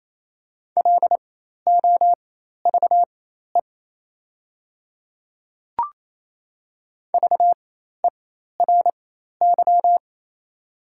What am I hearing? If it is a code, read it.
Morse code, heard as a keyed tone, about 28 wpm, spaced as for 13 wpm, 690 Hz. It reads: LOVE VERY
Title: 4x Character Spacing - Top 500 Words - 28wpm